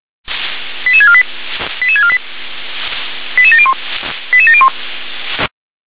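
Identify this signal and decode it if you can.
Five-tone selective call (ZVEI), heard as quick runs of five tones, tones of about 70 ms. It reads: 88648 88648 88701 88701
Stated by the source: ZVEI five-tone selective call received by SDR.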